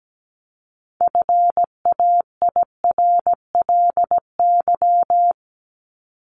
Morse code, heard as a keyed tone, about 17 wpm, 700 Hz. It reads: FAIRLY
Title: Top 1001-1250 Words 17wpm